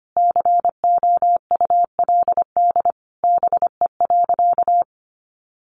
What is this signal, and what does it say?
Morse code, heard as a keyed tone, about 25 wpm, 700 Hz. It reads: COULD BE.
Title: Sentences from Top 1400 Words 25wpm